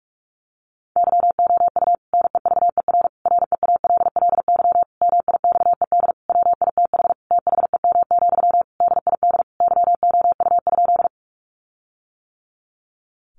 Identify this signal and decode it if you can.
Morse code, heard as a keyed tone, about 44 wpm, 700 Hz. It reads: YOU NEVER REALLY MIXED WITH THEM, DID YOU?
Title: Sentences from Top 1500 Words 45wpm